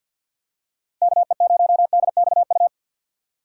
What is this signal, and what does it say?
Morse code, heard as a keyed tone, about 52 wpm, 700 Hz. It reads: K E 0 D X U
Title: USA Call Signs - 6 Characters - 8x Character Spacing - 50wpm